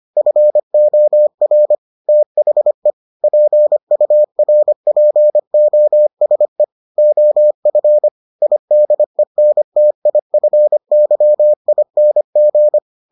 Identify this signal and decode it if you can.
Morse code, heard as a keyed tone, about 25 wpm, 600 Hz. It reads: FOR THE PURPOSE OF IDENTIFYING